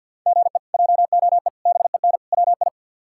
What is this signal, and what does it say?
Morse code, heard as a keyed tone, about 49 wpm, 700 Hz. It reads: GE JOE BEN WI
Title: K1USN SST QSO Practice - 50wpm